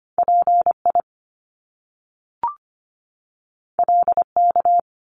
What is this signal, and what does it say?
Morse code, heard as a keyed tone, about 25 wpm, 700 Hz. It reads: PI LK
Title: Top Two Letter Combinations in Words 25wpm